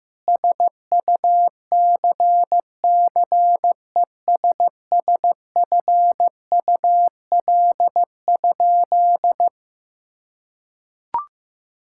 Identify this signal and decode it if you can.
Morse code, heard as a keyed tone, about 14 wpm, 700 Hz. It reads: SUCCESSFUL?
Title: Sentences from Top 1000 Words - Review Part 6 - 15wpm